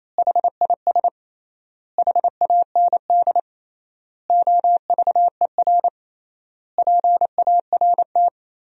Morse code, heard as a keyed tone, about 28 wpm, 700 Hz. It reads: HIS HAND OVER PART
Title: 3x Word-Spacing - Sentences from Top 400 Words - 28wpm